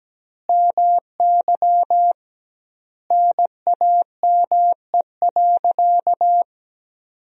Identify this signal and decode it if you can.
Morse code, heard as a keyed tone, about 17 wpm, 700 Hz. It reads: MY NAME.
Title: 2x Word-Spacing - Sentences from Top 500 Words - 17wpm